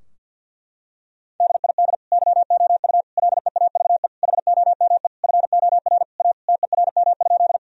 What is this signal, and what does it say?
Morse code, heard as a keyed tone, about 50 wpm, 700 Hz. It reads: DID YOU LEAVE HOME FOR A TERM?
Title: Sentences from Top 1000 Words - Review Part 4 - 50wpm